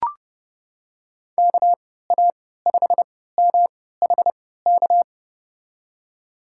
Morse code, heard as a keyed tone, about 30 wpm, spaced as for 17 wpm, 700 Hz. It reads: KA5MHK
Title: USA Call Signs - 6 Characters - 3x Character Spacing - 30wpm